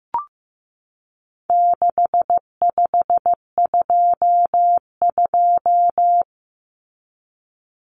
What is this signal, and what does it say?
Morse code, heard as a keyed tone, about 15 wpm, 700 Hz. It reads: 6522
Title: Common Cut Numbers - 4-Digit 15wpm